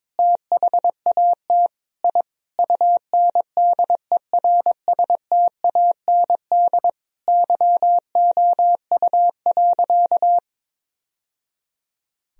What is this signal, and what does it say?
Morse code, heard as a keyed tone, about 22 wpm, 700 Hz. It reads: THAT I UNDERSTAND YOU.
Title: Sentences from Top 1000 Words - Review Part 2 - 22wpm